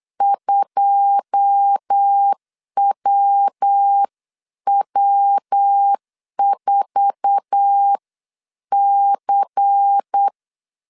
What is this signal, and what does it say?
Morse code, heard as a keyed tone, about 8 wpm, 800 Hz. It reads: 2WW4C
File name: CW / DET / 1 9 1500C S B 8wpm 5624